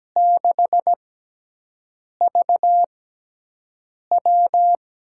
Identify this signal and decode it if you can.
Morse code, heard as a keyed tone, about 17 wpm, spaced as for 6 wpm, 700 Hz. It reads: 6VW
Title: USA Call Signs - 6x Character Spacing - 17wpm